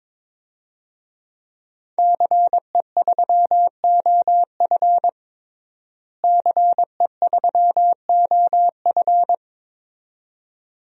Morse code, heard as a keyed tone, about 22 wpm, 700 Hz. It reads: CE3OF CE3OF
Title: DXCC-100 Call Signs - 5 Characters - Repeated 2x - 22wpm